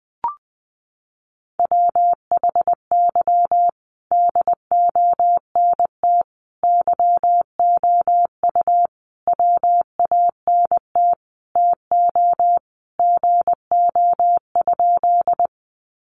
Morse code, heard as a keyed tone, about 20 wpm, 700 Hz. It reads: WHY DONT YOU WANT TO GO?
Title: Sentences from Top 300 Words 20wpm